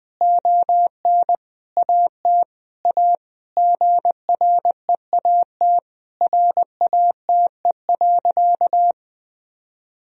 Morse code, heard as a keyed tone, about 21 wpm, 700 Hz. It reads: ON AT A GREAT RATE.
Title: Sentences from Top 1000 Words - Review Part 4 - 20wpm